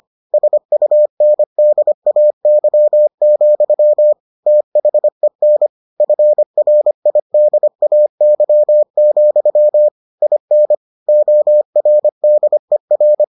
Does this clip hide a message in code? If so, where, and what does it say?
Yes: Morse code from 0.3 to 13.2 s, heard as a keyed tone, about 25 wpm, 600 Hz. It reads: SUNDAY, THEN FRIDAY, IN ORDER